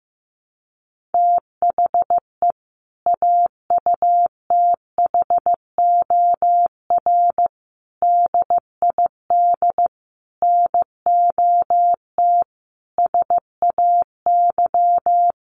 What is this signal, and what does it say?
Morse code, heard as a keyed tone, about 15 wpm, 700 Hz. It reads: THE AUTHOR DID NOT SAY
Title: Sentences from Top 1200 Words 15wpm